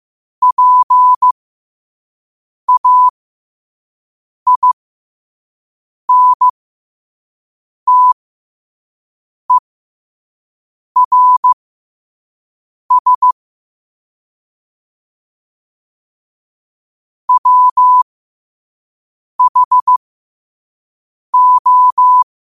Morse code, heard as a keyed tone, about 15 wpm, spaced as for 5 wpm, 1000 Hz. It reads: PAINTERS WHO
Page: New quotes every day in morse code at 5 Words per minute.